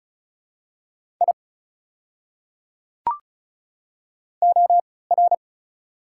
Morse code, heard as a keyed tone, about 36 wpm, spaced as for 20 wpm, 690 Hz. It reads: I OR
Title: 3x Character Spacing - Top 100 Words - 35wpm